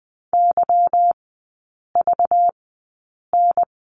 Morse code, heard as a keyed tone, about 20 wpm, spaced as for 8 wpm, 690 Hz. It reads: YVN
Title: 191 Lesson 31e - Letter v - Three Characters 20wpm